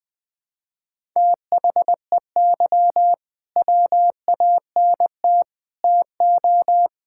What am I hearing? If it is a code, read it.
Morse code, heard as a keyed tone, about 20 wpm, 700 Hz. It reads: THEY WANT TO
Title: Sentences with 3-Word Phrases - Spoken Beginning - 20wpm